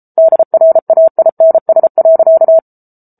Morse code, heard as a keyed tone, about 34 wpm, 650 Hz. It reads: DRAINS.